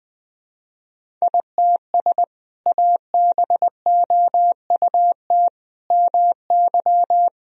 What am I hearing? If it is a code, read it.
Morse code, heard as a keyed tone, about 20 wpm, 700 Hz. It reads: ITS ABOUT MY
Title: Sentences from Top 1000 Words - Review Part 1 - 20wpm